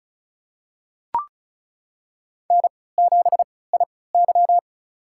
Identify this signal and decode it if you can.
Morse code, heard as a keyed tone, about 36 wpm, spaced as for 20 wpm, 700 Hz. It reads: N7IY
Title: USA Call Signs - 4 Characters - 3x Character Spacing - 35wpm